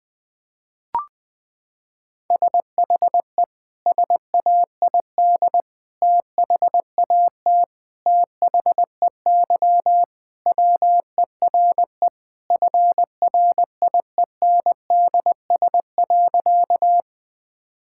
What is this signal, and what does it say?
Morse code, heard as a keyed tone, about 21 wpm, 700 Hz. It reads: SHE SAID THAT THEY WERE FRIENDS.
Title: Sentences with 3-Word Phrases - 20wpm